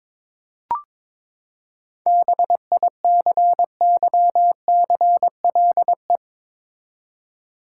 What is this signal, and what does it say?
Morse code, heard as a keyed tone, about 22 wpm, 700 Hz. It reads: BICYCLE
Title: Seven Letter Words - 22wpm